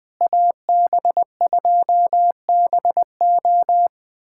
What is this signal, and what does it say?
Morse code, heard as a keyed tone, about 20 wpm, 700 Hz. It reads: AB2BO
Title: USA Callsign DE Callsign - 5 Characters - 20wpm